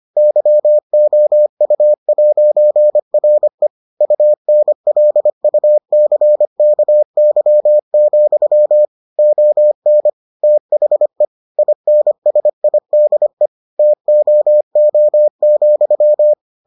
Morse code, heard as a keyed tone, about 25 wpm, 600 Hz. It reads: YOU'RE UNLUCKY, ON THE INSIDE TOO,